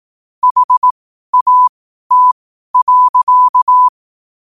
Morse code, heard as a keyed tone, about 18 wpm, spaced as for 12 wpm, 1000 Hz. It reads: HAT.
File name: qotdfiles / 12wpm